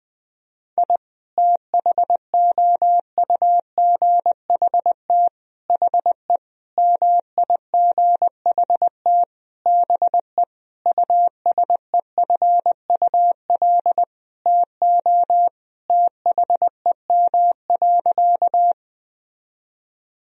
Morse code, heard as a keyed tone, about 20 wpm, 700 Hz. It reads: I THOUGHT HE MIGHT BE USEFUL TO THEM.
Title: Sentences from Top 1000 Words - Review Part 4 - 20wpm